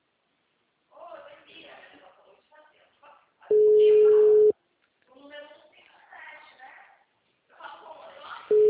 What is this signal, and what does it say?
Ringback tone (Europe, 425 Hz), with speech.
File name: out- -2010- - - .13091